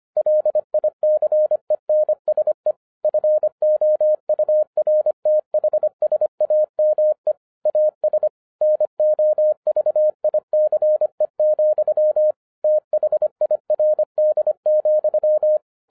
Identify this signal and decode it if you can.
Morse code, heard as a keyed tone, about 25 wpm, 610 Hz. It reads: LICENSE FOURTH SAME AS NOVICE, THIRD,